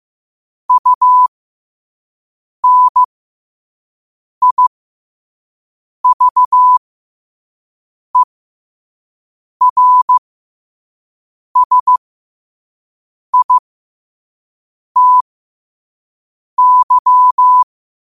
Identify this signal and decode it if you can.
Morse code, heard as a keyed tone, about 15 wpm, spaced as for 5 wpm, 1000 Hz. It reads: UNIVERSITY
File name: qotdfiles / 5wpm